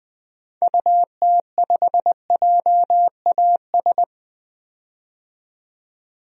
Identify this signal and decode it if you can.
Morse code, heard as a keyed tone, about 21 wpm, 700 Hz. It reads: UT5JAS
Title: DXCC-100 Call Signs - 6 Characters - 20wpm